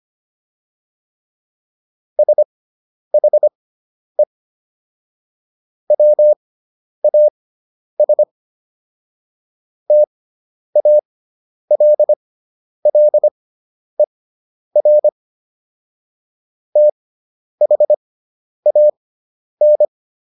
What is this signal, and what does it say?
Morse code, heard as a keyed tone, about 25 wpm, spaced as for 10 wpm, 600 Hz. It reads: SHE WAS TALLER THAN